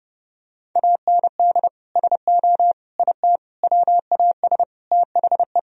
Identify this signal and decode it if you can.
Morse code, heard as a keyed tone, about 30 wpm, 700 Hz. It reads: AND SO IT WAS THE
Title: Sentences from Top 1000 Words - Review Part 1 - 30wpm